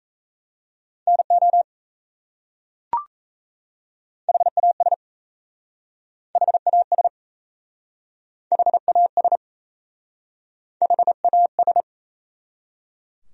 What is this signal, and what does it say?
Morse code, heard as a keyed tone, about 37 wpm, 700 Hz. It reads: NO HAS HAS HAS HAS